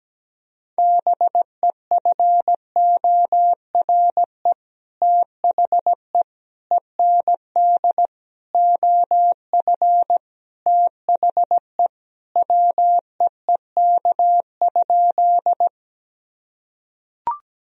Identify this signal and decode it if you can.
Morse code, heard as a keyed tone, about 17 wpm, 700 Hz. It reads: BEFORE THE END OF THE WEEK?
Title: Sentences from Top 1000 Words - Review Part 2 - 17wpm